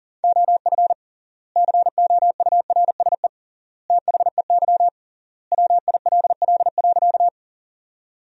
Morse code, heard as a keyed tone, about 37 wpm, 700 Hz. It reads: OF COURSE THEY WILL.
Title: 3x Word-Spacing - Sentences from Top 500 Words - 40wpm